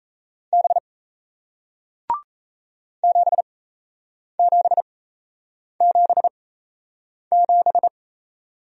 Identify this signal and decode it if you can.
Morse code, heard as a keyed tone, about 37 wpm, spaced as for 8 wpm, 700 Hz. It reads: B 7777